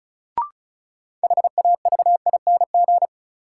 Morse code, heard as a keyed tone, about 36 wpm, 700 Hz. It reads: HAVING
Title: Sets of 5 Words Encore - New Word Drill - 35wpm